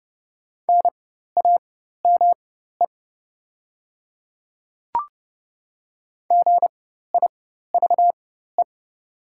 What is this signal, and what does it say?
Morse code, heard as a keyed tone, about 30 wpm, spaced as for 14 wpm, 700 Hz. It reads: NAME GIVE
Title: 4x Character Spacing - Top 200 Words - 30wpm